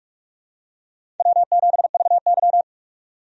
Morse code, heard as a keyed tone, about 46 wpm, 700 Hz. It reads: W7VY
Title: USA Call Signs - 4 Characters - 45wpm